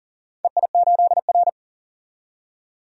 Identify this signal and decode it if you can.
Morse code, heard as a keyed tone, about 40 wpm, 700 Hz. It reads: EI8R
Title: DXCC-100 Call Signs - 4 Characters - 40wpm